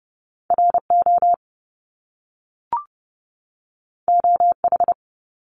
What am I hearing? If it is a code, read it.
Morse code, heard as a keyed tone, about 30 wpm, 700 Hz. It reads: RO OH
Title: Top Two Letter Combinations in Words 30wpm